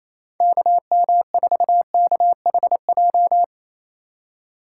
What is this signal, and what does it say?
Morse code, heard as a keyed tone, about 28 wpm, 700 Hz. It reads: KM4KHJ
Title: USA Call Signs - 6 Characters - 2x Character Spacing - 28wpm